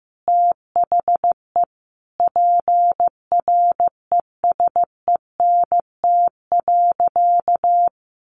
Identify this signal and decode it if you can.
Morse code, heard as a keyed tone, about 15 wpm, 700 Hz. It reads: THE PRESENT.
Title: Sentences from Top 800 Words 15wpm